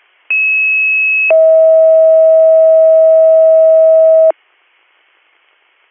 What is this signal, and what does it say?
Two-tone page, heard as a one-second tone, then a three-second tone, about 2600 then 640 Hz. It is heard from 0.3 to 4.3 s.